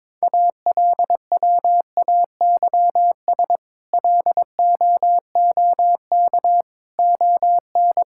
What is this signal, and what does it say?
Morse code, heard as a keyed tone, about 22 wpm, 700 Hz. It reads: ALWAYS LOOK ON
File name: practice / 22wpm / Sets of 8 Words - 22wpm